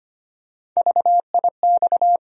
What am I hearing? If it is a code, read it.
Morse code, heard as a keyed tone, about 25 wpm, 700 Hz. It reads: VIX
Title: USA Call Sign Postfixes - 3 Letters 25wpm